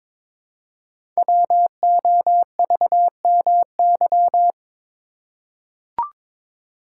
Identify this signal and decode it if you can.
Morse code, heard as a keyed tone, about 22 wpm, 700 Hz. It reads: WOVMY
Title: Code Groups - 5 Letters - 22wpm